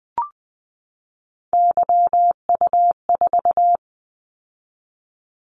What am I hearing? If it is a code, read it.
Morse code, heard as a keyed tone, about 20 wpm, 700 Hz. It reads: YU4
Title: Callsign Prefixes for Top 25 DXCC - 20wpm